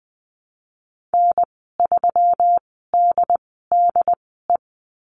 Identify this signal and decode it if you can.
Morse code, heard as a keyed tone, about 20 wpm, spaced as for 14 wpm, 700 Hz. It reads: N3DDE
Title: 167 Lesson 27g - Number 7 - Call Signs 20wpm